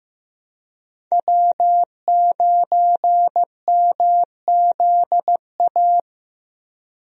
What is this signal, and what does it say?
Morse code, heard as a keyed tone, about 15 wpm, 700 Hz. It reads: W9MZA
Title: USA Call Signs - 5 Characters - Repeated 2x - 15wpm